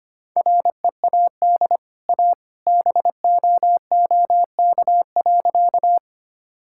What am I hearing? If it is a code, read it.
Morse code, heard as a keyed tone, about 25 wpm, 700 Hz. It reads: READ A BOOK.
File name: Easy Sentences with Hints - 25wpm